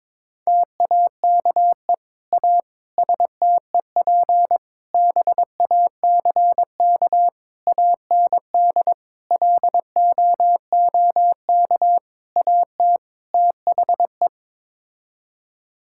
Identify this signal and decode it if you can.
Morse code, heard as a keyed tone, about 21 wpm, 700 Hz. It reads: TAKE A STEP BACK AND LOOK AT THE
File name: practice / 22wpm / Sets of 8 Words - 22wpm